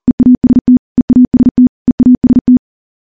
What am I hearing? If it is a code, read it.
Morse code, heard as a keyed tone, about 40 wpm, 250 Hz. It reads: EAST EAST EAST
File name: morse-training-40WPM